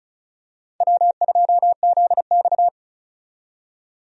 Morse code, heard as a keyed tone, about 34 wpm, 700 Hz. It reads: W2ZX
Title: USA Call Signs - 4 Characters - 35wpm